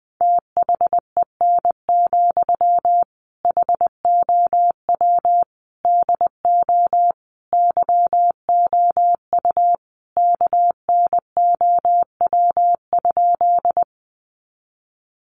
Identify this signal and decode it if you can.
Morse code, heard as a keyed tone, about 20 wpm, 700 Hz. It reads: THEN, HOW DO YOU KNOW?
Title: Sentences from Top 100 Words 20wpm